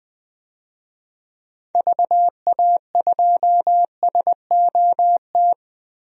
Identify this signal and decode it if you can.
Morse code, heard as a keyed tone, about 20 wpm, 700 Hz. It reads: VA2SOT
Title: DXCC-100 Call Signs - 6 Characters - 20wpm